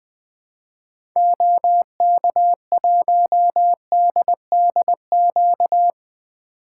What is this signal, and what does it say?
Morse code, heard as a keyed tone, about 21 wpm, 700 Hz. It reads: OK1DDQ
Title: 2023 SOTA Activator Call Signs - 20wpm